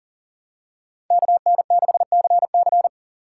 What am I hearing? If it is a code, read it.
Morse code, heard as a keyed tone, about 40 wpm, 700 Hz. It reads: KN6CC
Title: USA Call Signs - 5 Characters - 6x Character Spacing - 40wpm